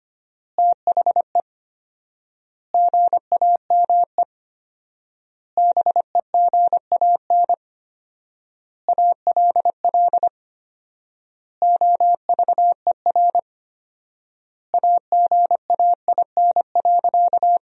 Morse code, heard as a keyed tone, about 25 wpm, 700 Hz. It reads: THE GAME BEGAN ALL OVER AGAIN.
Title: 4x Word-Spacing - Sentences from Top 500 Words - 25wpm